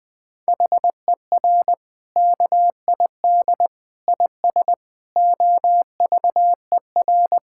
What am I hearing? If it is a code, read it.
Morse code, heard as a keyed tone, about 21 wpm, 700 Hz. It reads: HER KID IS OVER
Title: Sentences from Top 1600 Words 20wpm